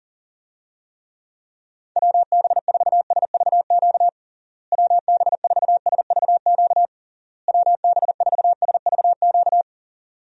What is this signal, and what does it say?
Morse code, heard as a keyed tone, about 40 wpm, 700 Hz. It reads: WB4SVQ WB4SVQ WB4SVQ
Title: USA Call Signs - 6 Characters - Repeated 3x - 40wpm